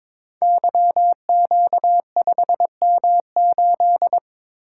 Morse code, heard as a keyed tone, about 23 wpm, 700 Hz. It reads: YQ5M8
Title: Code Groups - 5 Letter:Numbers - 22wpm